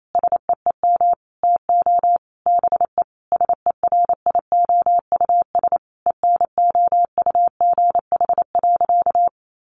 Morse code, heard as a keyed tone, about 28 wpm, 700 Hz. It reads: SEEM TO BE SERIOUS ENOUGH.
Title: Sentences from Top 1100 Words 28wpm